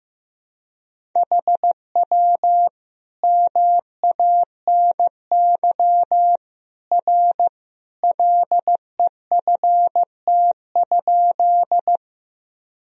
Morse code, heard as a keyed tone, about 15 wpm, 700 Hz. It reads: HW MANY R LEFT?
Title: Sentences from Top 300 Words and Abbreviations 15wpm